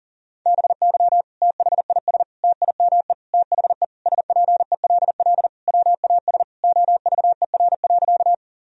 Morse code, heard as a keyed tone, about 40 wpm, 700 Hz. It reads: BY THIS TIME THE SPELL WAS OVER.
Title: Sentences from Top 1000 Words - Review Part 1 - 40wpm